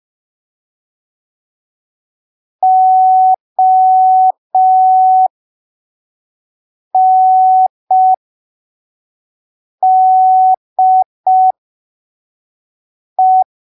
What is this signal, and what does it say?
Morse code, heard as a keyed tone, about 5 wpm, spaced as for 3 wpm, 750 Hz. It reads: ONDE
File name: arquivos / texto / texto 13 05ppm